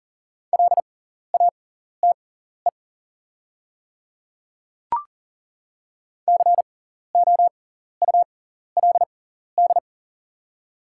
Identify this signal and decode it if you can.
Morse code, heard as a keyed tone, about 40 wpm, spaced as for 14 wpm, 690 Hz. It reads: LATE COULD
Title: 6x Character Spacing - Top 300 Words - 40wpm